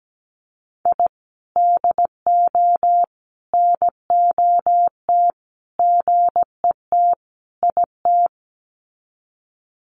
Morse code, heard as a keyed tone, about 17 wpm, 700 Hz. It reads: I DO NOT GET IT
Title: Sets of 5 Words Encore 17wpm